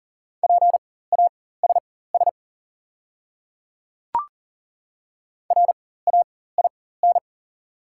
Morse code, heard as a keyed tone, about 38 wpm, spaced as for 18 wpm, 690 Hz. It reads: PASS RAIN
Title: 4x Character Spacing - Top 500 Words - 40wpm